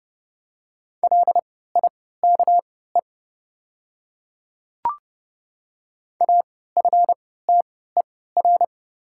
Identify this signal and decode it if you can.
Morse code, heard as a keyed tone, about 31 wpm, spaced as for 17 wpm, 700 Hz. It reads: LIKE AFTER
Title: 3x Character Spacing - Top 300 Words - 30wpm